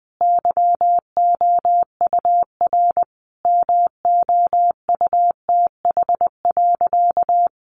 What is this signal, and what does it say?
Morse code, heard as a keyed tone, about 20 wpm, 700 Hz. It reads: YOUR MOUTH.
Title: Sentences from Top 1500 Words 20wpm